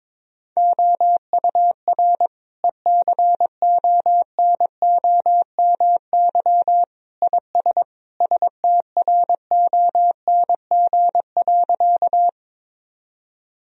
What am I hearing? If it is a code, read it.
Morse code, heard as a keyed tone, about 21 wpm, 700 Hz. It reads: OUR ECONOMY IS STRONG.
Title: Sentences from Top 1000 Words - Review Part 3 - 22wpm